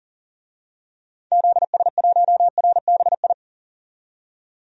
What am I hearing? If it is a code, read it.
Morse code, heard as a keyed tone, about 40 wpm, 700 Hz. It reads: ZS1RBI
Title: DXCC-100 Call Signs - 6 Characters - 40wpm